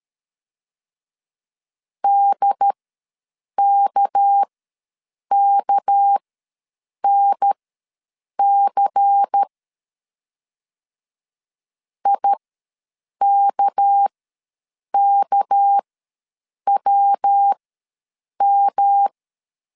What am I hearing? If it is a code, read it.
Morse code, heard as a keyed tone, about 13 wpm, spaced as for 7 wpm, 800 Hz. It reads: DKKNC IKKWM
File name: LEZIONI / 6 PKX0 / 6 PK ws 12wpm 3116